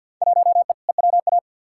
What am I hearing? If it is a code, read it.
Morse code, heard as a keyed tone, about 52 wpm, 700 Hz. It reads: 1E EWA
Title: Field Day Exchanges - 50wpm